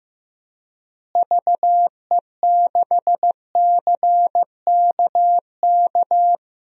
Morse code, heard as a keyed tone, about 15 wpm, 700 Hz. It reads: VE6CKK
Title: DXCC-100 Call Signs - 6 Characters - Repeated 3x - 15wpm